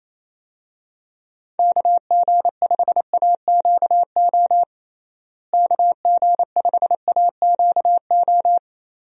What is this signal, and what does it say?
Morse code, heard as a keyed tone, about 28 wpm, 700 Hz. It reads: KG5AQO KG5AQO
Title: USA Call Signs - 6 Characters - Repeated 3x - 28wpm